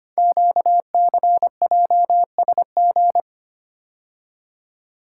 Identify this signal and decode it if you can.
Morse code, heard as a keyed tone, about 25 wpm, 700 Hz. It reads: QCJSG
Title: Code Groups - 5 Letters - 25wpm